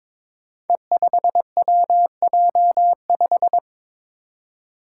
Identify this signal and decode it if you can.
Morse code, heard as a keyed tone, about 22 wpm, 700 Hz. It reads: E5WJ5
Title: Five Challenging Letter-Number Combinations 22wpm